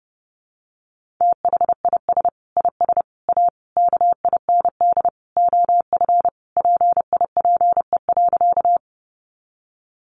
Morse code, heard as a keyed tone, about 30 wpm, 700 Hz. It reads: THIS IS A KIND OF PIPE.
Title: Sentences from Top 1500 Words 30wpm